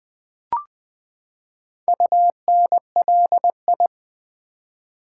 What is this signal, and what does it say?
Morse code, heard as a keyed tone, about 20 wpm, 700 Hz. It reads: UNLI
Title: Build A Word - 4 Letters At A Time - 8-Letter Words - 20wpm